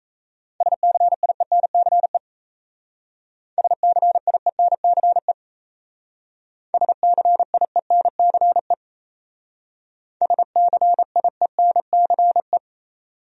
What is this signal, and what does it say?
Morse code, heard as a keyed tone, about 36 wpm, 700 Hz. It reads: SCIENCE SCIENCE SCIENCE SCIENCE